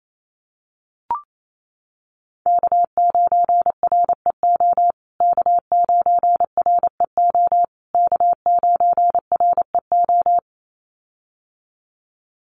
Morse code, heard as a keyed tone, about 28 wpm, 700 Hz. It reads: K9REO K9REO K9REO
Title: USA Call Signs Repeated 3x 28wpm